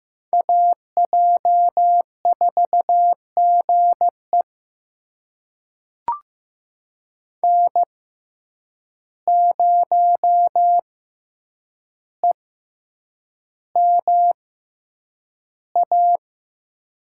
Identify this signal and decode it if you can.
Morse code, heard as a keyed tone, about 15 wpm, spaced as for 8 wpm, 700 Hz. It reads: AJ4GE N0EMA
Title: USA Call Signs - 5 Characters - 6x Character Spacing - 15wpm